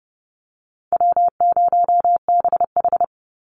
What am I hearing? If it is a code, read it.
Morse code, heard as a keyed tone, about 30 wpm, 700 Hz. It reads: W0BH
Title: Call Signs - 30wpm